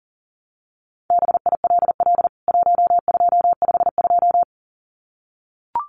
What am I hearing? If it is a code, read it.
Morse code, heard as a keyed tone, about 40 wpm, 700 Hz. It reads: BILL 1252
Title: CWT Exchange 40wpm